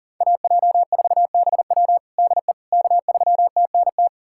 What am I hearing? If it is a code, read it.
Morse code, heard as a keyed tone, about 37 wpm, 700 Hz. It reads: AJ4BW DE K3TNT
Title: USA Callsign DE Callsign - 5 Characters - 40wpm